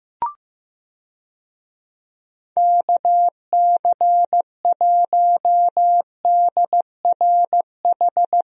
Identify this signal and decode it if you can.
Morse code, heard as a keyed tone, about 16 wpm, 700 Hz. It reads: KC1DRH
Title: USA Call Signs - 6 Characters - 15wpm